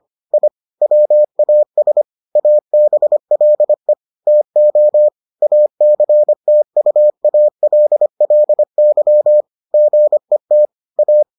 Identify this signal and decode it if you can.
Morse code, heard as a keyed tone, about 25 wpm, 600 Hz. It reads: I WAS ABLE TO ACTUALLY GET A